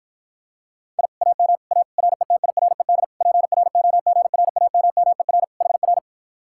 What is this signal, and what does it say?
Morse code, heard as a keyed tone, about 54 wpm, 700 Hz. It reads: I AM A RETIRED PROGRAMMER HR